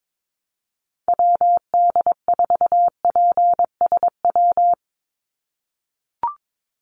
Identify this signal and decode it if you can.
Morse code, heard as a keyed tone, about 22 wpm, 700 Hz. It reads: WD4PSW
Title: USA Call Signs 22wpm